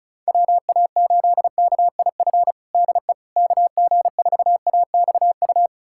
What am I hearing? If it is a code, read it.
Morse code, heard as a keyed tone, about 34 wpm, 700 Hz. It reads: WA8KIF DE KG4AXU
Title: USA Callsign DE Callsign - 6 Characters - 35wpm